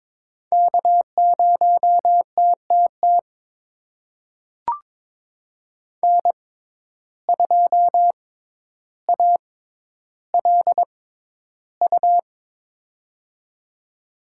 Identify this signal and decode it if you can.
Morse code, heard as a keyed tone, about 22 wpm, spaced as for 11 wpm, 700 Hz. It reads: K0TTT N2ALU
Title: USA Call Signs - 5 Characters - 6x Character Spacing - 22wpm